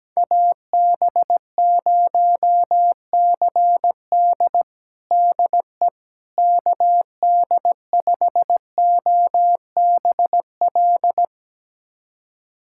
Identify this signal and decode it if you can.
Morse code, heard as a keyed tone, about 17 wpm, 700 Hz. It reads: AB0CD DE KD5OBL
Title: USA Callsign DE Callsign 17wpm